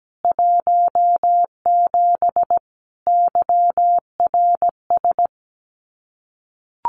Morse code, heard as a keyed tone, about 17 wpm, 700 Hz. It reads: 17 YRS
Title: Common QSO Phrases Encore 17wpm